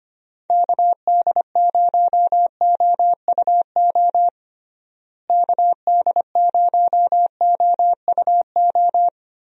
Morse code, heard as a keyed tone, about 25 wpm, 700 Hz. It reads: KD0OUO KD0OUO
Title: USA Call Signs - 6 Characters - Repeated 3x - 25wpm